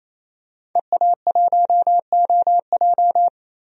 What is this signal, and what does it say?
Morse code, heard as a keyed tone, about 28 wpm, 700 Hz. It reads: EA1OJ
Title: DXCC-100 Call Signs - 5 Characters - 28wpm